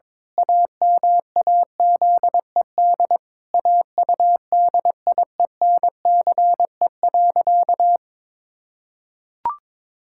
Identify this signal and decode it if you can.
Morse code, heard as a keyed tone, about 22 wpm, 700 Hz. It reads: AMAZED AUDIENCE.
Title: Pangram Challenge v2 - 22wpm